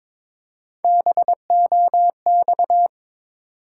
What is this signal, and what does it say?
Morse code, heard as a keyed tone, about 23 wpm, 700 Hz. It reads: BOX
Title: Three Letter Words - 22wpm